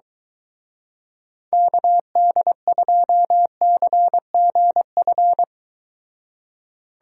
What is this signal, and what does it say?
Morse code, heard as a keyed tone, about 23 wpm, 700 Hz. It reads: KD2CGF